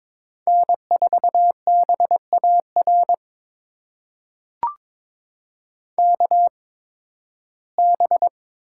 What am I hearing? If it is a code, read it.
Morse code, heard as a keyed tone, about 22 wpm, spaced as for 14 wpm, 700 Hz. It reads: N4BAR KB
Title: USA Call Signs - 8x Character Spacing - 22wpm